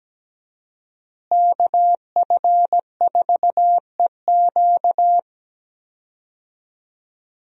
Morse code, heard as a keyed tone, about 17 wpm, 700 Hz. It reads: KF4EQ
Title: USA Call Signs - 5 Characters - Repeated 2x - 17wpm